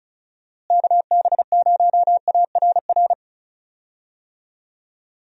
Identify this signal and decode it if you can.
Morse code, heard as a keyed tone, about 34 wpm, 700 Hz. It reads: KB0ARR
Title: USA Call Signs - 6 Characters - Repeated 3x - 35wpm